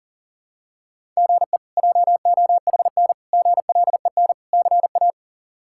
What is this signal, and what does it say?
Morse code, heard as a keyed tone, about 37 wpm, 700 Hz. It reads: GE JOHN GLEN CA
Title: K1USN SST QSO Practice - 40wpm